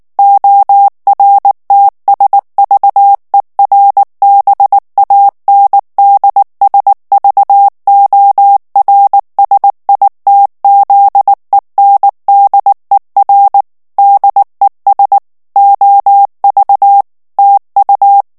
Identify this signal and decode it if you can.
Morse code, heard as a keyed tone, about 19 wpm, 800 Hz. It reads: ORTSVERBANDSVORSITZENDER DES OV TU
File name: morse mp3 / ALLEMAND texte clair 2 vitesse 18 mots minute